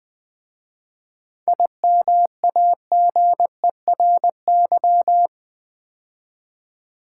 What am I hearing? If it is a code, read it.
Morse code, heard as a keyed tone, about 20 wpm, 700 Hz. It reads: IMAGERY
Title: 7 Letter Words - First 3 Letters Given - 20wpm